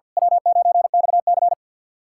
Morse code, heard as a keyed tone, about 49 wpm, 700 Hz. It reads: W9XC
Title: K1USN SST QSO Practice - 50wpm